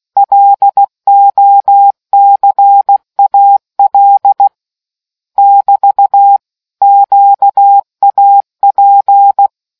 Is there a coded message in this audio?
Yes: Morse code from 0.2 to 9.5 s, heard as a keyed tone, about 16 wpm, 800 Hz. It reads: LOCAL <BT> QAP